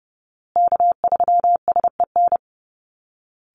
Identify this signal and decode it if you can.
Morse code, heard as a keyed tone, about 28 wpm, 700 Hz. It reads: K3SEN
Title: Call Signs Repeated 3x - 30wpm